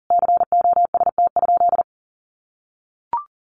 Morse code, heard as a keyed tone, about 40 wpm, 700 Hz. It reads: COST?
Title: Sentences from Top 1400 Words 40wpm